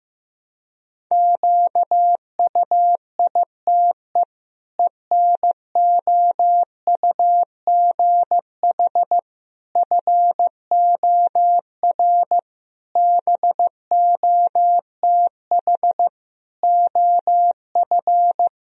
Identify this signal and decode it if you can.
Morse code, heard as a keyed tone, about 15 wpm, 700 Hz. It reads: QUITE ENOUGH FOR BOTH OF
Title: Sentences from Top 1000 Words - Review Part 1 - 15wpm